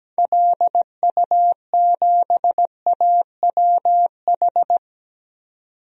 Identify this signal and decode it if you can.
Morse code, heard as a keyed tone, about 17 wpm, 700 Hz. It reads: LU7AWH
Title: DXCC-100 Call Signs - 6 Characters - 17wpm